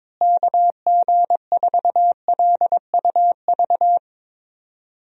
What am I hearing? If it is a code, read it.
Morse code, heard as a keyed tone, about 21 wpm, 700 Hz. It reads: KG4LUV
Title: USA Call Signs - 6 Characters - Repeated 3x - 22wpm